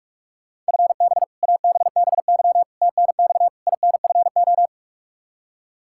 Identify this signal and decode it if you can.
Morse code, heard as a keyed tone, about 46 wpm, 700 Hz. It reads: FB ABBY TNX INFO
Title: Practice QSO - Exchange 3 - More Info - 45wpm